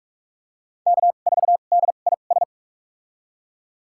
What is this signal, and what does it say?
Morse code, heard as a keyed tone, about 46 wpm, spaced as for 32 wpm, 690 Hz. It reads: K4DIS
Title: USA Call Signs - 5 Characters - 2x Character Spacing - 45wpm